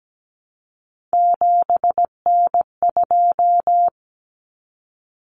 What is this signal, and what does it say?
Morse code, heard as a keyed tone, about 17 wpm, 700 Hz. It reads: 7N2
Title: Common Cut Numbers - 3-Digit 17wpm